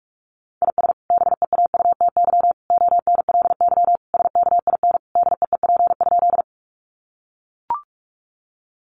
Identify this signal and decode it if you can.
Morse code, heard as a keyed tone, about 44 wpm, 700 Hz. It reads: IS BEAUTY ONLY SKIN DEEP?
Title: Sentences from Top 1300 Words 45wpm